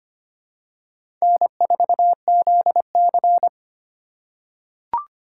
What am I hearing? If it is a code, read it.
Morse code, heard as a keyed tone, about 25 wpm, 700 Hz. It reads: N4ZC
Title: USA Call Signs - 4 Characters - 3x Character Spacing - 25wpm